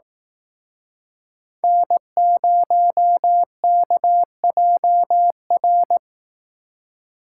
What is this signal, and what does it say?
Morse code, heard as a keyed tone, about 18 wpm, 700 Hz. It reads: N0KJR
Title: Speed Racing - USA 1x3 Call Signs 15wpm